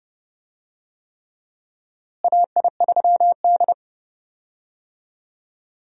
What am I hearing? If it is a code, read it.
Morse code, heard as a keyed tone, about 30 wpm, 700 Hz. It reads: AI3D
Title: USA Call Signs - 4 Characters - Repeated 3x - 30wpm